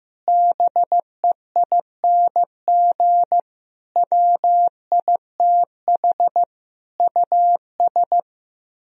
Morse code, heard as a keyed tone, about 15 wpm, 700 Hz. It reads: BEING WITH US
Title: Speed-Racing - Sets of 5 Words 15wpm